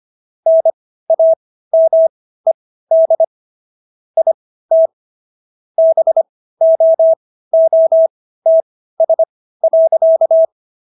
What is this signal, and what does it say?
Morse code, heard as a keyed tone, about 25 wpm, spaced as for 15 wpm, 650 Hz. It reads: NAMED IT BOOTS.